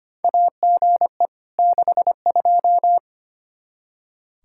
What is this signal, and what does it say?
Morse code, heard as a keyed tone, about 25 wpm, 700 Hz. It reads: AGE 62
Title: K1USN SST QSO Practice 25wpm